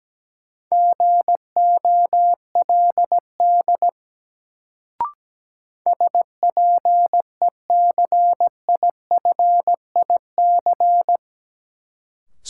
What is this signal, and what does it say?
Morse code, heard as a keyed tone, about 17 wpm, 700 Hz. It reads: GOLD SPECIFIC
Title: Sentences from Top 1000 Words - Review Part 3 - New Word Drill - 17wpm